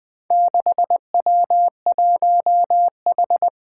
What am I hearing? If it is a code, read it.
Morse code, heard as a keyed tone, about 20 wpm, 700 Hz. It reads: 6W1H
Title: Four Challenging Letter-Number Combinations 20wpm